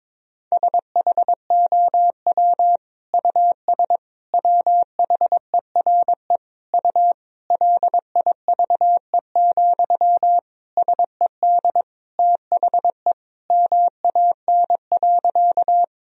Morse code, heard as a keyed tone, about 22 wpm, 700 Hz. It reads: SHOW US WHERE U LIVE, SED THE MAN.
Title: Sentences from Top 200 Words and Abbreviations 22wpm